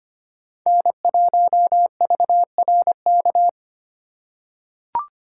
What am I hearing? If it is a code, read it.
Morse code, heard as a keyed tone, about 25 wpm, 700 Hz. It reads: N1VRK
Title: USA Call Signs - 5 Characters - 4x Character Spacing - 25wpm